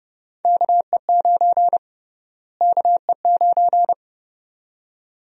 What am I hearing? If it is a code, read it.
Morse code, heard as a keyed tone, about 29 wpm, 700 Hz. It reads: KE9 KE9
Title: USA Call Sign Prefixes - Repeated 2x - 30wpm